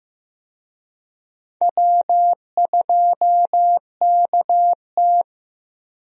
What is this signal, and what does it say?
Morse code, heard as a keyed tone, about 15 wpm, 700 Hz. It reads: W2KT
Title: USA Call Signs - 4 Characters - 15wpm